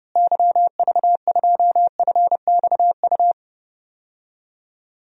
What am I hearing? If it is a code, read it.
Morse code, heard as a keyed tone, about 31 wpm, 700 Hz. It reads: YV2FXU
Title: DXCC-100 Call Signs - 6 Characters - Repeated 2x - 30wpm